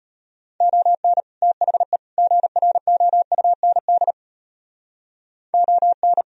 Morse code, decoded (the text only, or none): ON THE GROUND ON